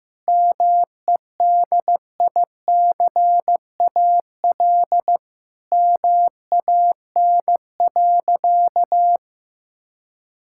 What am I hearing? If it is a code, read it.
Morse code, heard as a keyed tone, about 16 wpm, 700 Hz. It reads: MEDICAL MAN.
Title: Sentences from Top 1000 Words - Review Part 5 - 15wpm